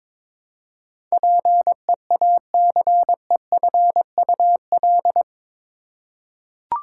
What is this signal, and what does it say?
Morse code, heard as a keyed tone, about 22 wpm, 700 Hz. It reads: PEACEFUL
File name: practice / 22wpm / Eight Letter Words - 22wpm